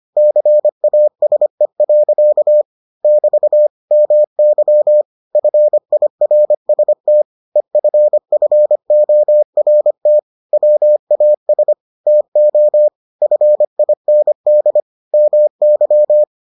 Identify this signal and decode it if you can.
Morse code, heard as a keyed tone, about 25 wpm, 600 Hz. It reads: CASE. <BT> MY FIRST EFFORT WAS TO FIND MY